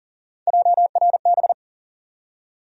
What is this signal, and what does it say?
Morse code, heard as a keyed tone, about 40 wpm, 700 Hz. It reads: JRB
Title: USA Call Sign Postfixes - 3 Letters 40wpm